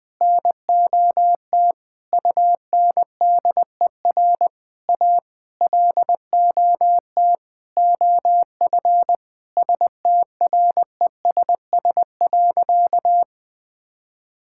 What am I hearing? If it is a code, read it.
Morse code, heard as a keyed tone, about 20 wpm, 700 Hz. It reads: NOT UNDER A LOT OF STRESS.
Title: Sentences from Top 1000 Words - Review Part 6 - 20wpm